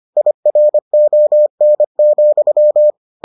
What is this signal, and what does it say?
Morse code, heard as a keyed tone, about 25 wpm, 600 Hz. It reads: IRON,